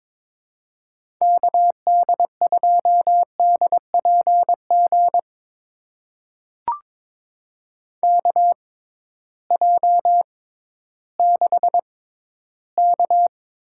Morse code, heard as a keyed tone, about 22 wpm, spaced as for 13 wpm, 700 Hz. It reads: KD2DPG KJ6K
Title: USA Call Signs - 6 Characters - 6x Character Spacing - 22wpm